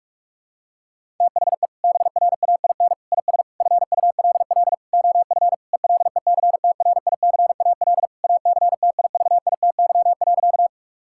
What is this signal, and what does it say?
Morse code, heard as a keyed tone, about 46 wpm, 700 Hz. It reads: THE BRAIN IS FULL OF ELECTRICAL ACTIVITY.
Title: Sentences from Top 1000 Words - Review Part 6 - 45wpm